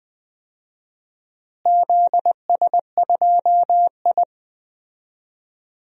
Morse code, heard as a keyed tone, about 20 wpm, 700 Hz. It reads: ZS2I
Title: DXCC-100 Call Signs - 4 Characters - Repeated 2x - 20wpm